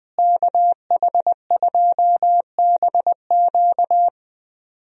Morse code, heard as a keyed tone, about 20 wpm, 700 Hz. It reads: KH2BQ
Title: USA Call Signs - 5 Characters - Repeated 2x - 20wpm